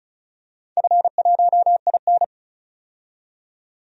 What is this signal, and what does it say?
Morse code, heard as a keyed tone, about 34 wpm, 700 Hz. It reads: F1IN
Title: DXCC-100 Call Signs - 4 Characters - 35wpm